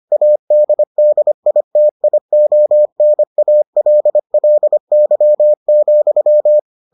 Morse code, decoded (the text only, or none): ADDITIONALLY,